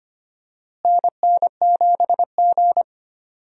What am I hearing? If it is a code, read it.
Morse code, heard as a keyed tone, about 25 wpm, 700 Hz. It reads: NN7G
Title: USA Call Signs - 4 Characters - Repeated 2x - 25wpm